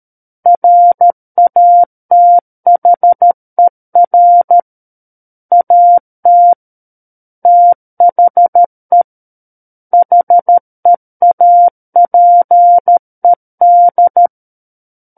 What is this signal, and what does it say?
Morse code, heard as a keyed tone, about 13 wpm, 700 Hz. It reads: RATHER AT THE HEAPED
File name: Book One / 04-Chapter 4 - 13 WPM